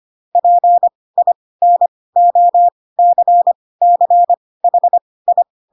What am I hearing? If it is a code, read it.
Morse code, heard as a keyed tone, about 25 wpm, spaced as for 18 wpm, 700 Hz. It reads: PINOCCHI
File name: Pinocchio ch3 12 25 ik1hge